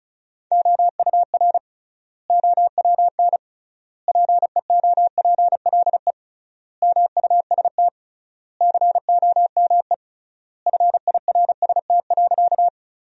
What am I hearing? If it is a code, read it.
Morse code, heard as a keyed tone, about 36 wpm, 700 Hz. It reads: OUR OWN PEOPLE MUST COME FIRST.
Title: 3x Word-Spacing - Sentences from Top 300 Words - 35wpm